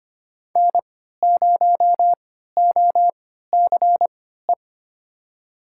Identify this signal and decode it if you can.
Morse code, heard as a keyed tone, about 25 wpm, spaced as for 14 wpm, 700 Hz. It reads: N0OCE
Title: USA Call Signs - 5 Characters - 3x Character Spacing - 25wpm